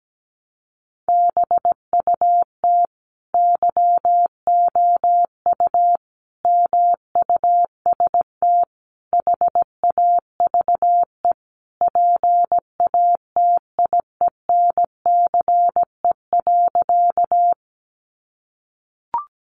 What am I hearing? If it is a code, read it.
Morse code, heard as a keyed tone, about 17 wpm, 700 Hz. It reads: BUT YOU MUST HAVE PATIENCE.
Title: Sentences from Top 1300 Words 17wpm